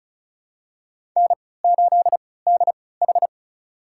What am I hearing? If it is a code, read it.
Morse code, heard as a keyed tone, about 36 wpm, spaced as for 20 wpm, 700 Hz. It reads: N8DH
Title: USA Call Signs - 4 Characters - 3x Character Spacing - 35wpm